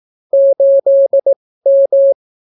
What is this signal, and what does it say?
Morse code, heard as a keyed tone, about 18 wpm, spaced as for 14 wpm, 550 Hz. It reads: 8M